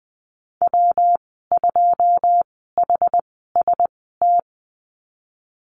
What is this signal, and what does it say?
Morse code, heard as a keyed tone, about 20 wpm, spaced as for 14 wpm, 700 Hz. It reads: W2HST
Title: 128 Lesson 21g - Number 6 - Call Signs 20wpm